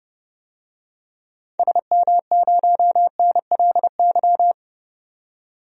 Morse code, heard as a keyed tone, about 30 wpm, 700 Hz. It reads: SM0NLY
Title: DXCC-100 Call Signs - 6 Characters - 30wpm